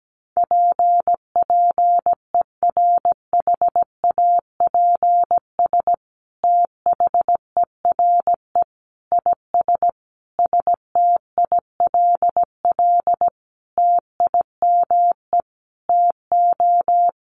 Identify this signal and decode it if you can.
Morse code, heard as a keyed tone, about 17 wpm, 700 Hz. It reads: PPERHAPS THERE IS STILL TIME TO
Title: Sentences from Top 1100 Words 17wpm